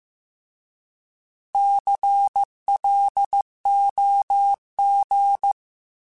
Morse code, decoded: CLOG

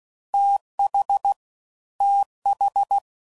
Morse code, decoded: TH TH

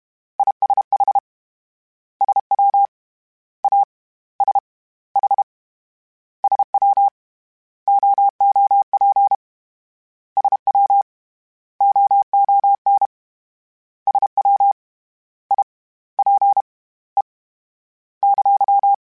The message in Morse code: ISH SW A S H SW OOP SW OON SW I P E !

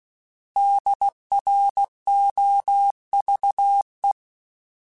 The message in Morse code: DROVE

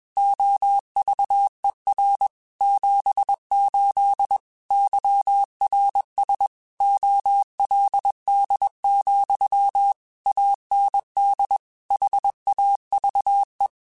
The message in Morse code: OVER 78 YRS OLD, AND HAVE

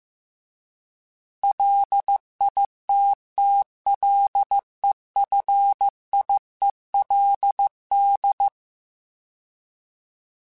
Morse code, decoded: LITTLEFIELD

